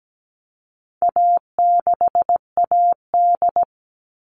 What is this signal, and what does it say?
Morse code, heard as a keyed tone, about 17 wpm, 700 Hz. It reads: A6AD
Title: All Cut Numbers - 4-Digit 17wpm